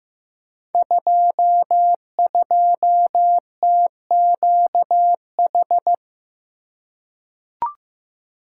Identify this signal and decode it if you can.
Morse code, heard as a keyed tone, about 15 wpm, 700 Hz. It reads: 22TQH
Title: Code Groups - 5 Letter:Numbers - 15wpm